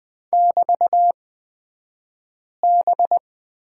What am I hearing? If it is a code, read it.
Morse code, heard as a keyed tone, about 20 wpm, spaced as for 5 wpm, 700 Hz. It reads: <BT>B
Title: 254 Lesson 41d - Period - Two Characters 20wpm